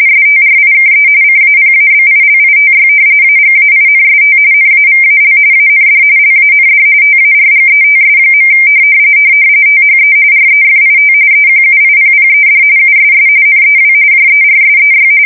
RTTY, about 45.45 baud. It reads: STRIKE CAPABILITY AGAINST NAVAL AND LAND TARGETS. THE REFIT EQUIPPED USS NEW JERSEY
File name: fsk170 / battleshipnj